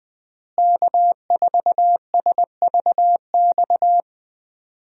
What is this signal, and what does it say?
Morse code, heard as a keyed tone, about 20 wpm, 700 Hz. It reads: K4SVX
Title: USA Call Signs - 5 Characters - Repeated 3x - 20wpm